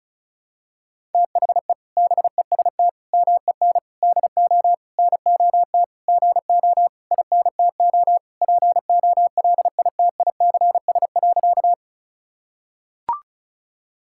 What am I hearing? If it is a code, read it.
Morse code, heard as a keyed tone, about 36 wpm, 700 Hz. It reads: THE BEST MEN DO NOT GO INTO POLITICS.
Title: Sentences from Top 1000 Words - Review Part 6 - 35wpm